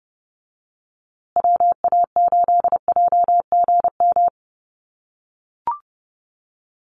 Morse code, heard as a keyed tone, about 30 wpm, 700 Hz. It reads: WA8JGM
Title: USA Call Signs Repeated 2x 30wpm